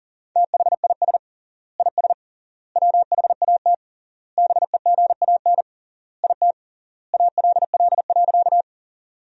Morse code, heard as a keyed tone, about 40 wpm, 700 Hz. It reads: THIS IS WHAT BEGAN IT ALL.
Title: 3x Word-Spacing - Sentences from Top 300 Words - 40wpm